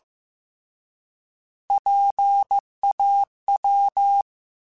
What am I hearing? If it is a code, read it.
Morse code, heard as a keyed tone, about 15 wpm, 800 Hz. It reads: PAW